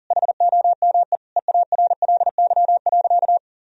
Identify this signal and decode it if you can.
Morse code, heard as a keyed tone, about 40 wpm, 700 Hz. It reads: HOME EARLY.
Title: Sentences from Top 1000 Words - Review Part 4 - 40wpm